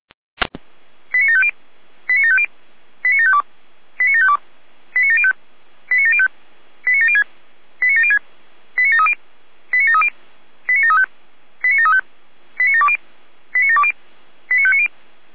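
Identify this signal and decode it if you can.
Five-tone selective call (ZVEI), heard as quick runs of five tones, tones of about 70 ms. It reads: 89750 89750 89752 89752 89705 89705 89706 89706 89730 89730 89735 89735 89720 89720 89590
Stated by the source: ZVEI five-tone selective call received on SDR.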